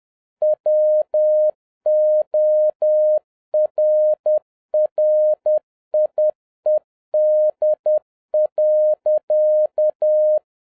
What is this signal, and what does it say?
Morse code, heard as a keyed tone, about 10 wpm, 610 Hz. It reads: WORRIED.